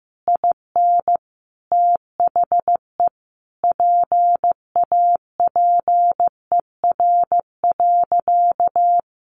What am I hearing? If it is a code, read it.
Morse code, heard as a keyed tone, about 15 wpm, 700 Hz. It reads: IN THE PAPER.
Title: Sentences from Top 300 Words 15wpm